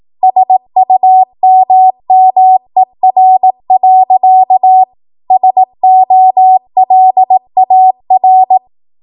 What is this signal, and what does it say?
Morse code, heard as a keyed tone, about 18 wpm, 750 Hz. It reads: SUMMER. SOLAR
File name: Morse / 18WPM